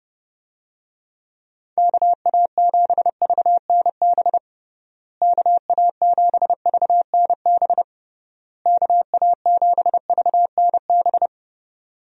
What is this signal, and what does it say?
Morse code, heard as a keyed tone, about 30 wpm, 700 Hz. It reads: KA7VNB KA7VNB KA7VNB
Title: USA Call Signs - 6 Characters - Repeated 3x - 30wpm